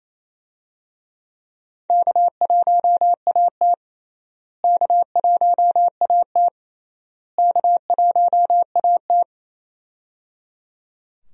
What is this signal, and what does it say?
Morse code, heard as a keyed tone, about 27 wpm, 700 Hz. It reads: K1AT K1AT K1AT
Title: USA Call Signs - 4 Characters - Repeated 3x - 28wpm